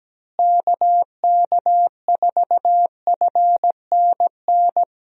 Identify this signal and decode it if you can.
Morse code, heard as a keyed tone, about 17 wpm, 700 Hz. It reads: KK4FNN
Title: USA Call Signs - 6 Characters - Repeated 2x - 17wpm